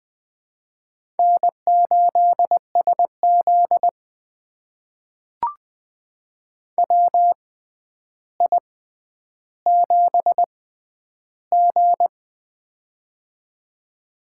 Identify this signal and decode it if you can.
Morse code, heard as a keyed tone, about 20 wpm, spaced as for 10 wpm, 700 Hz. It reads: N8SZ WI7G
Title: USA Call Signs - 4 Characters - 6x Character Spacing - 20wpm